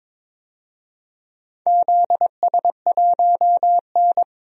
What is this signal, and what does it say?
Morse code, heard as a keyed tone, about 21 wpm, 700 Hz. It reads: ZS1N
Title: DXCC-100 Call Signs - 4 Characters - 22wpm